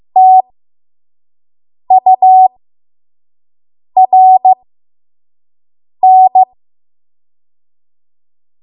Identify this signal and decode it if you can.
Morse code, heard as a keyed tone, about 15 wpm, spaced as for 5 wpm, 750 Hz. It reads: TURN
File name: Morse / 05WPM